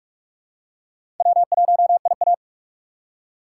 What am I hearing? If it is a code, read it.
Morse code, heard as a keyed tone, about 44 wpm, 700 Hz. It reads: W1IA
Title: USA Call Signs - 4 Characters - 3x Character Spacing - 45wpm